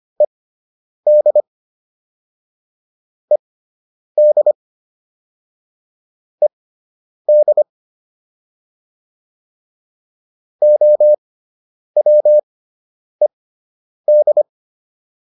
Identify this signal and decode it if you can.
Morse code, heard as a keyed tone, about 25 wpm, spaced as for 9 wpm, 600 Hz. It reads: ED ED ED OWED